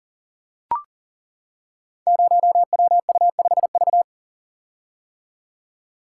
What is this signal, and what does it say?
Morse code, heard as a keyed tone, about 39 wpm, 700 Hz. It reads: 0WU5V
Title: Code Groups - 5 Letter:Numbers - 40wpm